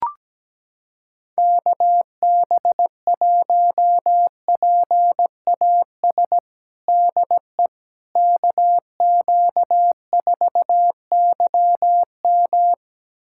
Morse code, heard as a keyed tone, about 17 wpm, 700 Hz. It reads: KB1PAS DE KQ4YM
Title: USA Callsign DE Callsign 17wpm